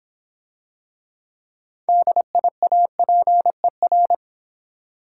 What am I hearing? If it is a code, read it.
Morse code, heard as a keyed tone, about 26 wpm, 700 Hz. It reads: DIAPER